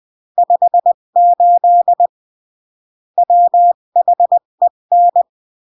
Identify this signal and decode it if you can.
Morse code, heard as a keyed tone, about 20 wpm, 700 Hz. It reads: 58 WHEN
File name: CW / jodorowsky / 82 commandments 20-15